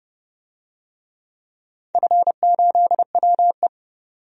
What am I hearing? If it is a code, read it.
Morse code, heard as a keyed tone, about 32 wpm, 700 Hz. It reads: F8WE
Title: DXCC-100 Call Signs - 4 Characters - 30wpm